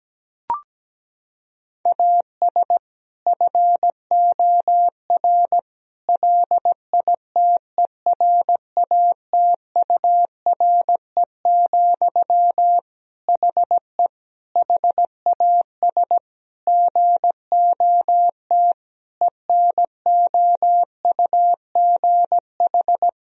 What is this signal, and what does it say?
Morse code, heard as a keyed tone, about 17 wpm, 700 Hz. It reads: AS FOR LITERATURE, HE HAS GOT ENOUGH
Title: Sentences from Top 1000 Words - Review Part 3 - 17wpm